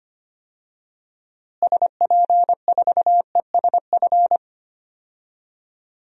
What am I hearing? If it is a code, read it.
Morse code, heard as a keyed tone, about 25 wpm, 700 Hz. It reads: SP4ESF
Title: DXCC-100 Call Signs - 6 Characters - Repeated 3x - 25wpm